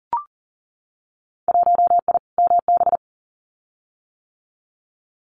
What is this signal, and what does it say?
Morse code, heard as a keyed tone, about 38 wpm, 700 Hz. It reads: 1I MB
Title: Winter Field Day Exchanges Encore 40wpm